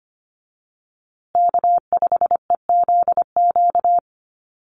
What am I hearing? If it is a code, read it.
Morse code, heard as a keyed tone, about 25 wpm, 700 Hz. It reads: K5EZQ
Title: USA Call Signs Repeated 2x 25wpm